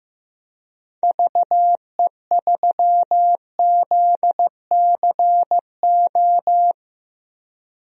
Morse code, heard as a keyed tone, about 15 wpm, 700 Hz. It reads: VE3ZCO
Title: DXCC-100 Call Signs - 6 Characters - 15wpm